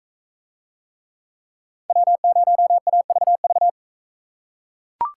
Morse code, heard as a keyed tone, about 42 wpm, 700 Hz. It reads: W0AVV